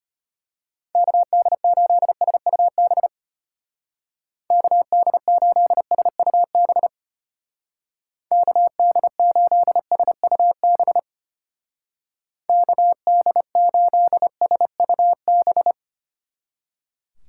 Morse code, heard as a keyed tone, about 32 wpm, 700 Hz. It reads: KD8SUB KD8SUB KD8SUB KD8SUB